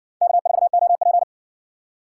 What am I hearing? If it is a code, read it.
Morse code, heard as a keyed tone, about 57 wpm, 700 Hz. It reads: K4OP